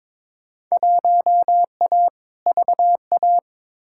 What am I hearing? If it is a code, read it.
Morse code, heard as a keyed tone, about 22 wpm, 700 Hz. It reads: 1A VA
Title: Field Day Exchanges - 22wpm